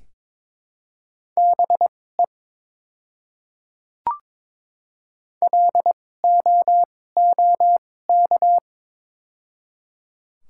Morse code, heard as a keyed tone, about 22 wpm, spaced as for 16 wpm, 700 Hz. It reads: BE LOOK
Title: 2x Character Spacing - Top 100 Words - 22wpm